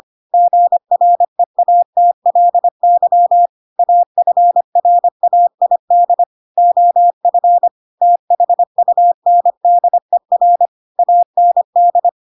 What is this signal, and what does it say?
Morse code, heard as a keyed tone, about 25 wpm, 700 Hz. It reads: GREATLY AFRAID OF THUNDER AND
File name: Pinocchio / Pinocchio ch6 25 25 ik1hge